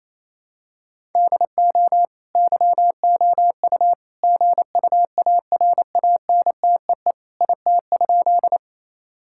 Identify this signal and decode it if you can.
Morse code, heard as a keyed tone, about 28 wpm, 700 Hz. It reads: DO YOU GUARANTEE IT?
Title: Sentences from Top 1600 Words 28wpm